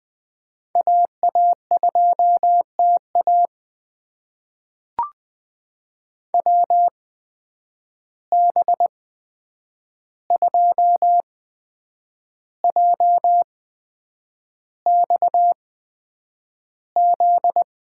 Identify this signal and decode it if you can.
Morse code, heard as a keyed tone, about 20 wpm, spaced as for 8 wpm, 700 Hz. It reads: AA2TA WB2JXZ
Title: USA Call Signs - 8x Character Spacing - 20wpm